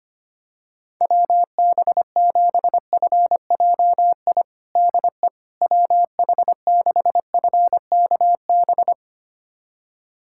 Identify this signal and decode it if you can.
Morse code, heard as a keyed tone, about 25 wpm, 700 Hz. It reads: WB7FJI DE WH6FKB
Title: USA Callsign DE Callsign - 6 Characters - 25wpm